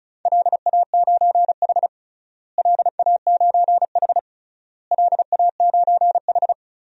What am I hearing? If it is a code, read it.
Morse code, heard as a keyed tone, about 36 wpm, 700 Hz. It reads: LA9H LA9H LA9H
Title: DXCC-100 Call Signs - 4 Characters - Repeated 3x - 35wpm